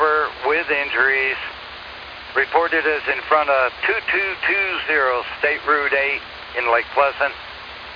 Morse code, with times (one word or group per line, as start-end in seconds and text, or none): none